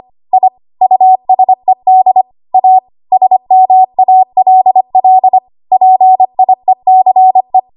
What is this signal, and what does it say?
Morse code, heard as a keyed tone, about 25 wpm, 750 Hz. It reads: I USED A SMALL PIECE
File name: Archive / 25 WPM / 25WPM